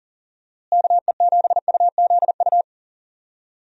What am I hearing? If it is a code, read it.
Morse code, heard as a keyed tone, about 40 wpm, 700 Hz. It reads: KE7UZU
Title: USA Call Signs - 6 Characters - Repeated 3x - 40wpm